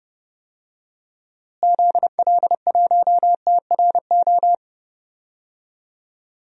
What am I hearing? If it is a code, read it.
Morse code, heard as a keyed tone, about 32 wpm, 700 Hz. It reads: ZL1TRO
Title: DXCC-100 Call Signs - 6 Characters - Repeated 2x - 30wpm